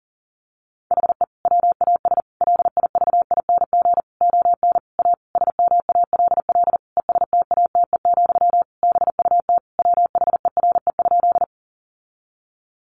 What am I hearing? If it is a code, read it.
Morse code, heard as a keyed tone, about 40 wpm, 700 Hz. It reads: HE WAS LIVING ON A SMALL ESTATE, BUT WHERE?
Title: Sentences from Top 1200 Words 40wpm